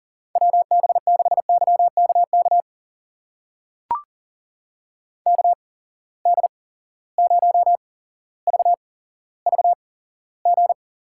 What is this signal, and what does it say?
Morse code, heard as a keyed tone, about 42 wpm, spaced as for 17 wpm, 700 Hz. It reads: WB6YKK KD0VVG
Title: USA Call Signs - 6 Characters - 8x Character Spacing - 40wpm